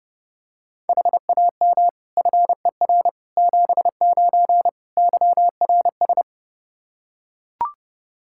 Morse code, heard as a keyed tone, about 32 wpm, 700 Hz. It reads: HAM FER 79 YRS
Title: Practice QSO - Exchange 3 - More Info - 30wpm